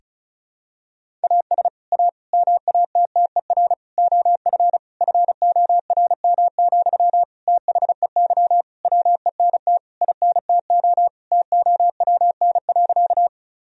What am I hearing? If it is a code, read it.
Morse code, heard as a keyed tone, about 36 wpm, 700 Hz. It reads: AS A MATTER OF FORM, THEY WENT INTO TOWN.
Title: Sentences from Top 1000 Words - Review Part 1 - 35wpm